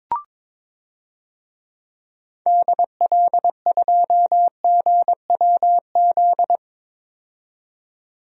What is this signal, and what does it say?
Morse code, heard as a keyed tone, about 22 wpm, 700 Hz. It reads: DL2GWZ
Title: DXCC-100 Call Signs - 6 Characters - 22wpm